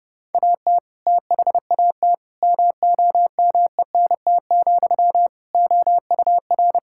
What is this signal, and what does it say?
Morse code, heard as a keyed tone, about 32 wpm, 700 Hz. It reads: AT THAT MOMENT, OUR
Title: Sentences from Top 1600 Words 30wpm